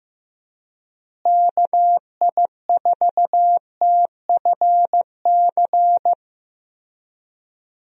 Morse code, heard as a keyed tone, about 15 wpm, 700 Hz. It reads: KI4TFC
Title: USA Call Signs - 6 Characters - Repeated 3x - 15wpm